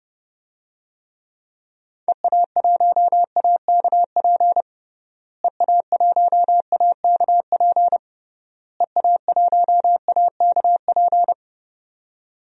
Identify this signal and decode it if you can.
Morse code, heard as a keyed tone, about 30 wpm, 700 Hz. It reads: EA1AKP EA1AKP EA1AKP
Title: DXCC-100 Call Signs - 6 Characters - Repeated 3x - 30wpm